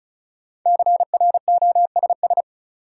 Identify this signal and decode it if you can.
Morse code, heard as a keyed tone, about 35 wpm, 700 Hz. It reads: CROSS
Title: Sentences from Top 300 Words - New Word Drill - 35wpm